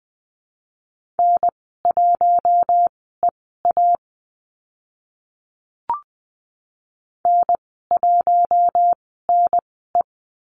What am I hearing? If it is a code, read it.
Morse code, heard as a keyed tone, about 20 wpm, spaced as for 15 wpm, 700 Hz. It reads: N1EA N1NE
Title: 042 Lesson 8g - Number 1 - Call Signs 20wpm